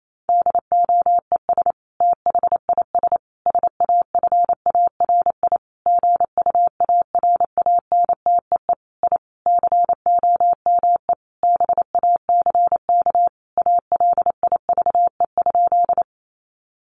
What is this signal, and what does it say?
Morse code, heard as a keyed tone, about 28 wpm, 700 Hz. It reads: DOES THIS SAFARI GUARANTEE I COME BACK ALIVE?
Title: A Sound Of Thunder 28wpm